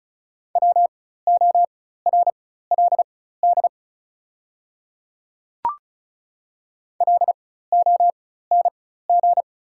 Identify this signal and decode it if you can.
Morse code, heard as a keyed tone, about 36 wpm, spaced as for 16 wpm, 700 Hz. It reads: WORLD LONG
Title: 4x Character Spacing - Top 300 Words - 35wpm